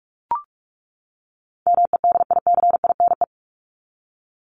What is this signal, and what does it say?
Morse code, heard as a keyed tone, about 44 wpm, 700 Hz. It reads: MEDICINE
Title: Top 751-1000 Words 45wpm